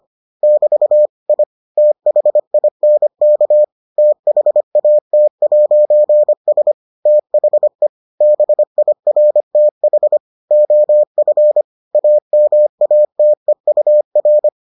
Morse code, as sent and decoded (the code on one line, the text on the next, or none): -...- / .. / - .... .. -. -.- / - .... .- - .----. ... / - .... / -... .. .-. - .... / --- ..-. / .- -- .- - . ..- .-.
<BT> I THINK THAT'S THE BIRTH OF AMATEUR